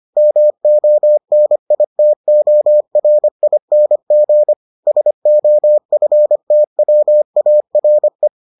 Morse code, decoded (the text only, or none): MONITORING SOFTWARE